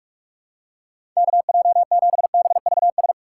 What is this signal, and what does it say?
Morse code, heard as a keyed tone, about 46 wpm, 700 Hz. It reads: KJ7BVS
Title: USA Call Signs - 6 Characters - 6x Character Spacing - 45wpm